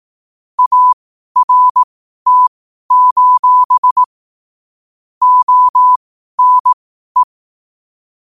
Morse code, heard as a keyed tone, about 18 wpm, spaced as for 12 wpm, 1000 Hz. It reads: ART: ONE